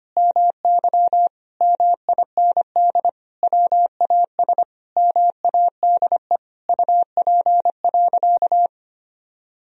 Morse code, heard as a keyed tone, about 25 wpm, 700 Hz. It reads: MY MIND WAS MADE UP.
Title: Sentences from Top 1000 Words - Review Part 2 - 25wpm